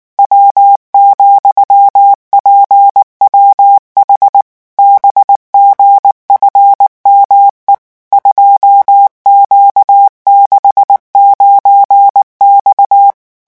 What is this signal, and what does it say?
Morse code, heard as a keyed tone, about 19 wpm, 800 Hz. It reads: W,PWH BGFME 2Q69X
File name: morse mp3 / revision13-vitesse 18 mots